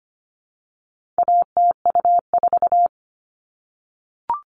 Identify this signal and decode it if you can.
Morse code, heard as a keyed tone, about 25 wpm, 700 Hz. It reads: ATU4
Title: All Cut Numbers - 4-Digit 25wpm